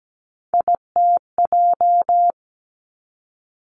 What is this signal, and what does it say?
Morse code, heard as a keyed tone, about 17 wpm, 700 Hz. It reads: ITJ
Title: Three Challenging Letter-Number Combinations 17wpm